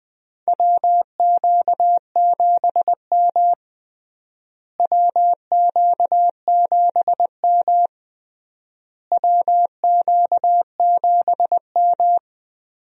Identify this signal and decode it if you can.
Morse code, heard as a keyed tone, about 20 wpm, 700 Hz. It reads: WQ7M WQ7M WQ7M
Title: USA Call Signs - 4 Characters - Repeated 3x - 20wpm